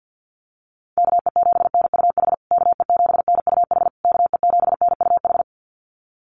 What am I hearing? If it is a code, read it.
Morse code, heard as a keyed tone, about 50 wpm, 700 Hz. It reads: KE7NUH KE7NUH KE7NUH
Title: USA Call Signs Repeated 3x 50wpm